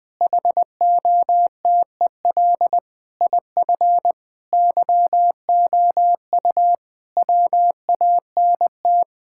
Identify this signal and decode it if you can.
Morse code, heard as a keyed tone, about 20 wpm, 700 Hz. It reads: HOTEL IF YOU WANT
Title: Sentences from Top 1600 Words 20wpm